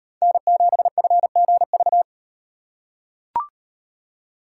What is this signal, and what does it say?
Morse code, heard as a keyed tone, about 38 wpm, 700 Hz. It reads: N7FGV